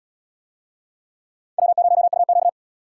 Morse code, heard as a keyed tone, about 69 wpm, 700 Hz. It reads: W0UQ